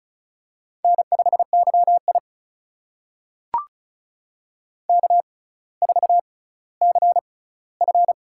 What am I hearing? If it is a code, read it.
Morse code, heard as a keyed tone, about 36 wpm, spaced as for 18 wpm, 700 Hz. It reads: N5YI K4CF
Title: USA Call Signs - 4 Characters - 6x Character Spacing - 35wpm